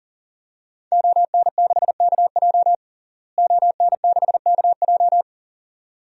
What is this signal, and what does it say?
Morse code, heard as a keyed tone, about 40 wpm, 700 Hz. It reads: ON6KJ ON6KJ
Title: DXCC-100 Call Signs - 5 Characters - Repeated 2x - 40wpm